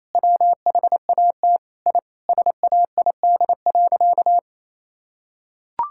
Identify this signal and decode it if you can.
Morse code, heard as a keyed tone, about 27 wpm, 700 Hz. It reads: WHAT I SAID.
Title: Sentences from Top 1000 Words - Review Part 6 - 28wpm